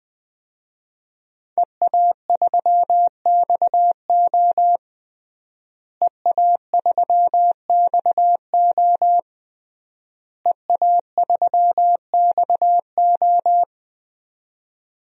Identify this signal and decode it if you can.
Morse code, heard as a keyed tone, about 20 wpm, 700 Hz. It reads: EA3XO EA3XO EA3XO
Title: DXCC-100 Call Signs - 5 Characters - Repeated 3x - 20wpm